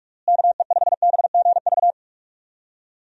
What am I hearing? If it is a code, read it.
Morse code, heard as a keyed tone, about 46 wpm, 700 Hz. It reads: KE5BGV
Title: USA Call Signs - 6 Characters - Repeated 3x - 45wpm